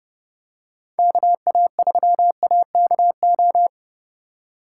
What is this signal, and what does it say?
Morse code, heard as a keyed tone, about 28 wpm, 700 Hz. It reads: KA3AKO